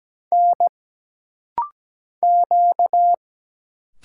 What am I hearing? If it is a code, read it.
Morse code, heard as a keyed tone, about 17 wpm, 700 Hz. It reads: N Q
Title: Single Letters - Rapid-Fire 17wpm